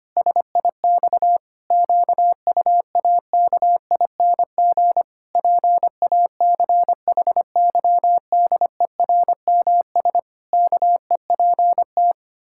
Morse code, heard as a keyed tone, about 25 wpm, 700 Hz. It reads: SIX QUAKING PACHYDERMS KEPT
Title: Pangram Challenge v2 - 25wpm